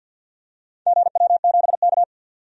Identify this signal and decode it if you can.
Morse code, heard as a keyed tone, about 51 wpm, 700 Hz. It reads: GW7K
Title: DXCC-100 Call Signs - 4 Characters - 50wpm